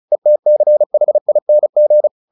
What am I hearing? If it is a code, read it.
Morse code, heard as a keyed tone, about 34 wpm, 600 Hz. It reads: ETCHING
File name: the old man and the sea / the old man and the sea 1 35wpm